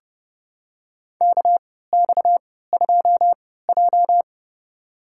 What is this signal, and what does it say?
Morse code, heard as a keyed tone, about 30 wpm, spaced as for 17 wpm, 700 Hz. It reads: KX2J
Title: USA Call Signs - 4 Characters - 3x Character Spacing - 30wpm